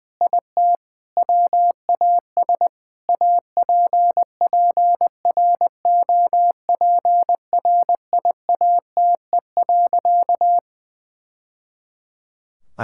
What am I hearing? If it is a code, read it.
Morse code, heard as a keyed tone, about 20 wpm, 700 Hz. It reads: IT WAS APPROPRIATE.
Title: Sentences from Top 1000 Words - Review Part 5 - 20wpm